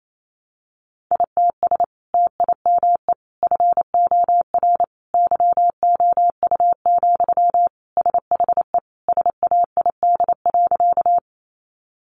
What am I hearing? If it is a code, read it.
Morse code, heard as a keyed tone, about 28 wpm, 700 Hz. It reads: ITS TIME FOR YOU, SHE SAID.
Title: Sentences from Top 600 Words 28wpm